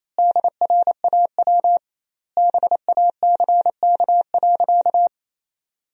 Morse code, heard as a keyed tone, about 28 wpm, 700 Hz. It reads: DRAW BACK.
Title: 2x Word-Spacing - Sentences from Top 300 Words - 28wpm